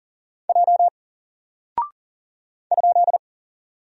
Morse code, heard as a keyed tone, about 39 wpm, 700 Hz. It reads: J ?
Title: Single Character (Letter:Number:Punctuation:Prosign) 40wpm